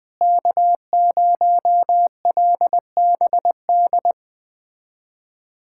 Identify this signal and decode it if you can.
Morse code, heard as a keyed tone, about 20 wpm, 700 Hz. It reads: K0LBD
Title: USA Callsign DE Callsign - 5 Characters - 20wpm